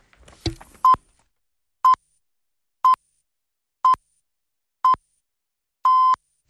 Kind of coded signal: radio time signal